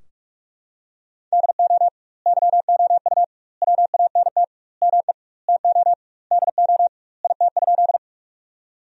Morse code, heard as a keyed tone, about 41 wpm, 700 Hz. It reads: DO YOU WANT ME TO DO IT?
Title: 2x Word-Spacing - Sentences from Top 200 Words - 45wpm